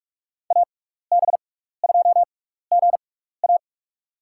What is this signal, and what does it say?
Morse code, heard as a keyed tone, about 46 wpm, spaced as for 16 wpm, 700 Hz. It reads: AB2GA
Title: USA Call Signs - 5 Characters - 6x Character Spacing - 45wpm